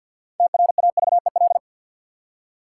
Morse code, heard as a keyed tone, about 53 wpm, 700 Hz. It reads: TRAVEL
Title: Six Letter Words - 50wpm